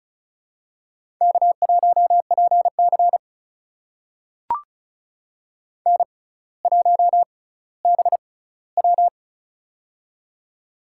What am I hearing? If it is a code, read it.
Morse code, heard as a keyed tone, about 36 wpm, spaced as for 18 wpm, 700 Hz. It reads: K1PC N1BW
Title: USA Call Signs - 4 Characters - 6x Character Spacing - 35wpm